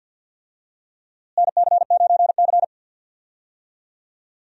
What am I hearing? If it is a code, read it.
Morse code, heard as a keyed tone, about 47 wpm, 700 Hz. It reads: NC9C
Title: USA Call Signs - 4 Characters - Repeated 3x - 50wpm